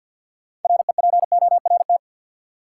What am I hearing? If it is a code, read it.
Morse code, heard as a keyed tone, about 53 wpm, 700 Hz. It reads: REPORT
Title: Sentences from Top 900 Words - New Word Drill - 50wpm